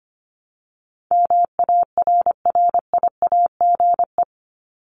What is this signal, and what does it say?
Morse code, heard as a keyed tone, about 25 wpm, 700 Hz. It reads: MARRIAGE
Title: Top 751-1000 Words 25wpm